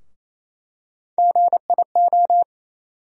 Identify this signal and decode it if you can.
Morse code, heard as a keyed tone, about 28 wpm, 700 Hz. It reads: GIO
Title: USA Call Sign Postfixes - 3 Letters 28wpm